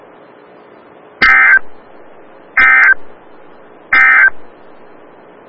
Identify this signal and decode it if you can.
Emergency Alert System (SAME) tones, heard as harsh buzzing bursts. It is heard from 1.1 to 4.4 s.
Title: Des Moines, IA NOAA Weather Radio - WXL57